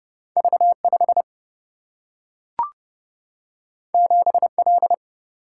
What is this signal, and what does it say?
Morse code, heard as a keyed tone, about 30 wpm, 700 Hz. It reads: V5 7L
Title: Single-Letter Single-Number 30wpm